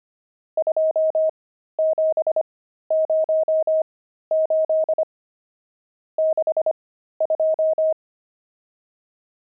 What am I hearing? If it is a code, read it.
Morse code, heard as a keyed tone, about 25 wpm, spaced as for 13 wpm, 630 Hz. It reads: 2708 62
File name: pangram-fs-13-cs-25-nums12